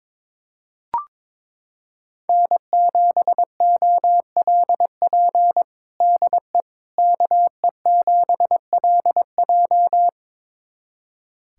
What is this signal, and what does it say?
Morse code, heard as a keyed tone, about 22 wpm, 700 Hz. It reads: N7OLP DE KE7LJ
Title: USA Callsign DE Callsign - 5 Characters - 22wpm